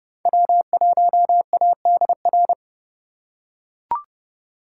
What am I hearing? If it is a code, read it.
Morse code, heard as a keyed tone, about 28 wpm, 700 Hz. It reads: W1ADR